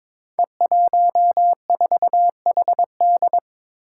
Morse code, heard as a keyed tone, about 22 wpm, 700 Hz. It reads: E14HD
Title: Five Challenging Letter-Number Combinations 22wpm